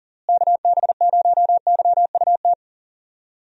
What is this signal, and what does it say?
Morse code, heard as a keyed tone, about 40 wpm, 700 Hz. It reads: KB0YUT
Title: USA Call Signs - 6 Characters - Repeated 2x - 40wpm